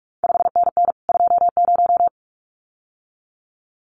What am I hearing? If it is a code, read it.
Morse code, heard as a keyed tone, about 46 wpm, 700 Hz. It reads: 5NN 20
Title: CQ WW DX Contest Exchange Encore - 45wpm